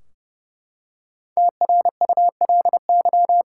Morse code, heard as a keyed tone, about 30 wpm, 700 Hz. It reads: TRULY
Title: Five Letter Words - 30wpm